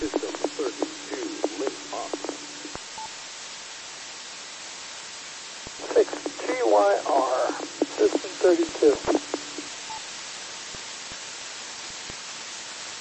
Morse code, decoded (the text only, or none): none